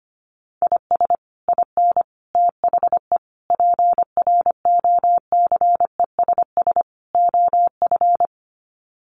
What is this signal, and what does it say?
Morse code, heard as a keyed tone, about 25 wpm, 700 Hz. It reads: IS IN THE PROCESS OF
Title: Sets of 5 Words Encore2 25wpm